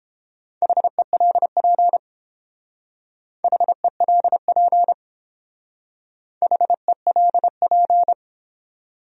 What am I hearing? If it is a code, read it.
Morse code, heard as a keyed tone, about 29 wpm, 700 Hz. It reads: HELP HELP HELP